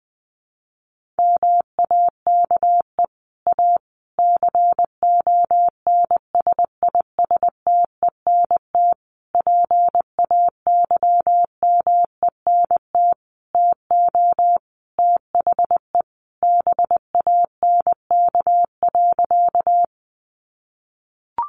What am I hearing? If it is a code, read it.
Morse code, heard as a keyed tone, about 20 wpm, 700 Hz. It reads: MAKE A CONSISTENT PAYMENT TO THE BANK.
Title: Sentences from Top 1500 Words 20wpm